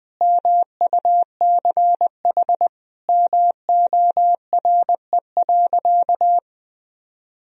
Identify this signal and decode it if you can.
Morse code, heard as a keyed tone, about 20 wpm, 700 Hz. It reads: MUCH MORE.
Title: Sentences from Top 1000 Words - Review Part 4 - 20wpm